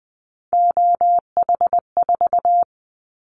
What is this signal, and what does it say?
Morse code, heard as a keyed tone, about 20 wpm, 700 Hz. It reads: OH4
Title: Callsign Prefixes for Top 50 DXCC - 20wpm